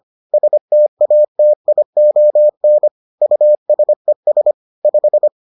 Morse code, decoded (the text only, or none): STATION USES 5